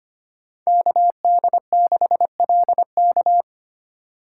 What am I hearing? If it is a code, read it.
Morse code, heard as a keyed tone, about 25 wpm, 700 Hz. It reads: KD6LK
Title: USA Call Signs - 5 Characters - 2x Character Spacing - 25wpm